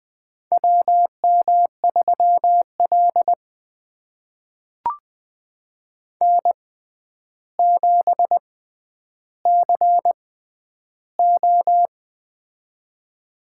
Morse code, heard as a keyed tone, about 20 wpm, spaced as for 10 wpm, 700 Hz. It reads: WM3L N7CO
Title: USA Call Signs - 4 Characters - 6x Character Spacing - 20wpm